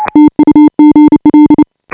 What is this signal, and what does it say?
Morse code, heard as a keyed tone, about 30 wpm, 310 Hz. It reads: TU GL
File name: AUDIO2018 / HA5BA CW